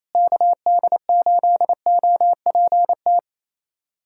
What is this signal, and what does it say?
Morse code, heard as a keyed tone, about 28 wpm, 700 Hz. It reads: KD8OPT
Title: USA Callsign DE Callsign 28wpm